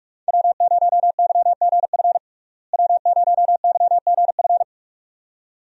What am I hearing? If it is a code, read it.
Morse code, heard as a keyed tone, about 46 wpm, 700 Hz. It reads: W0YGF W0YGF
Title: USA Call Signs - 5 Characters - Repeated 2x - 45wpm